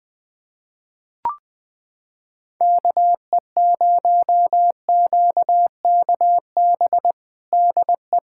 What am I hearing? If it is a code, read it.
Morse code, heard as a keyed tone, about 20 wpm, 700 Hz. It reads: KE0QKB DE
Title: USA Callsign DE Callsign - 6 Characters - 20wpm